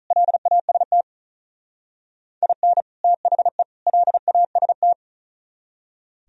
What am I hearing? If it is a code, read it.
Morse code, heard as a keyed tone, about 45 wpm, 700 Hz. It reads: LAST IN THE LAST